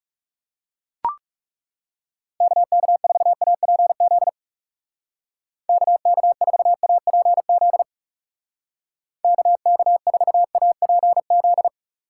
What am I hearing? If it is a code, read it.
Morse code, heard as a keyed tone, about 36 wpm, 700 Hz. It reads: KK4APZ KK4APZ KK4APZ
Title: Speed Racing - USA 2x3 Call Signs 30wpm